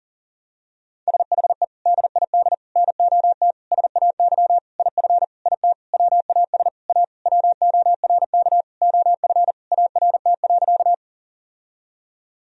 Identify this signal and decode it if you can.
Morse code, heard as a keyed tone, about 40 wpm, 700 Hz. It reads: SHE DID NOT SAY IF IT WAS A WORK OF ART.
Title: Sentences from Top 1000 Words - Review Part 3 - 40wpm